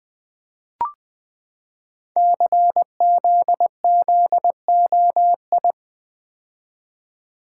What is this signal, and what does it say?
Morse code, heard as a keyed tone, about 20 wpm, 700 Hz. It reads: CZZOI
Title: Code Groups - 5 Letters - 20wpm